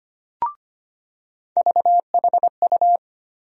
Morse code, heard as a keyed tone, about 25 wpm, 700 Hz. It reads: VHU
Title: USA Call Sign Postfixes - 3 Letters 25wpm